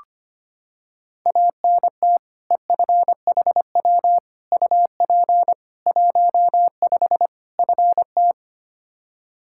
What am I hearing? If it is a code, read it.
Morse code, heard as a keyed tone, about 25 wpm, 700 Hz. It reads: ANT EFHW UP 15 FT
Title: Practice QSO - Exchange 2 - Info - 25wpm